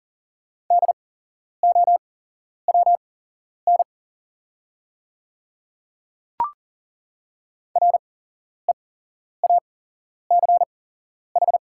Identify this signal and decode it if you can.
Morse code, heard as a keyed tone, about 40 wpm, spaced as for 11 wpm, 700 Hz. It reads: DOWN REACH
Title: 8x Character Spacing - Top 500 Words - 40wpm